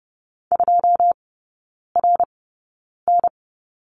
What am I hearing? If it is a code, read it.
Morse code, heard as a keyed tone, about 30 wpm, spaced as for 9 wpm, 700 Hz. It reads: 2RN
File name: course / 30:10wpm / 087 Lesson 15e - Number 5 - Three Characters 30wpm